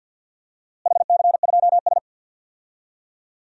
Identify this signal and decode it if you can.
Morse code, heard as a keyed tone, about 47 wpm, 700 Hz. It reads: HC2S
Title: DXCC-100 Call Signs - 4 Characters - Repeated 2x - 50wpm